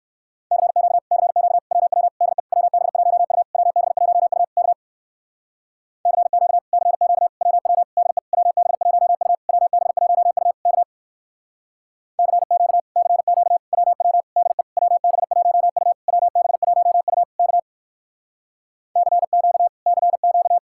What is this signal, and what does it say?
Morse code, heard as a keyed tone, about 58 wpm, 700 Hz. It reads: CQ CQ WW DE WB1U WB1U K CQ CQ WW DE WB1U WB1U K CQ CQ WW DE WB1U WB1U K CQ CQ